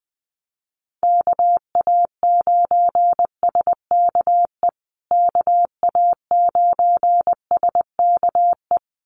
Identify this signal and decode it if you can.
Morse code, heard as a keyed tone, about 20 wpm, 700 Hz. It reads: KA9SKE KA9SKE
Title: USA Call Signs Repeated 2x 20wpm